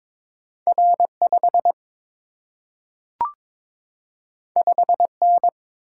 Morse code, heard as a keyed tone, about 22 wpm, 700 Hz. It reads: R5 5N
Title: Single-Letter Single-Number 22wpm